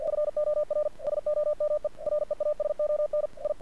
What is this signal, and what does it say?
Morse code, heard as a keyed tone, about 48 wpm, 600 Hz. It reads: FOR SOME REASON I
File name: archived / CRSnet